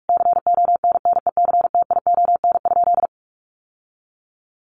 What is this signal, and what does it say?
Morse code, heard as a keyed tone, about 44 wpm, 700 Hz. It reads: CONNECTION?
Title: Sentences from Top 1300 Words 45wpm